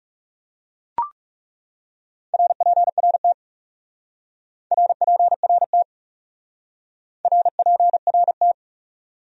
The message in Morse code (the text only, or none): RPRT RPRT RPRT